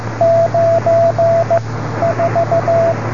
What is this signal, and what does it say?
Morse code, heard as a keyed tone, about 15 wpm, 690 Hz, with radio noise. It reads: 9 4